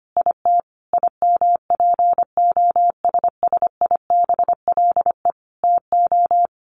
Morse code, heard as a keyed tone, about 25 wpm, 700 Hz. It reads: IT IMPOSSIBLE TO
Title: Sentences from Top 1100 Words 25wpm